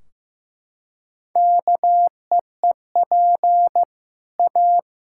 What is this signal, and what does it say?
Morse code, heard as a keyed tone, about 16 wpm, 700 Hz. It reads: KEEP A
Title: Sentences from Top 1600 Words 15wpm